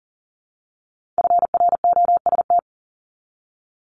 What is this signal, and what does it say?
Morse code, heard as a keyed tone, about 39 wpm, 690 Hz. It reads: FROST
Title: Multi-letter Phonemes Repeated 3x 40wpm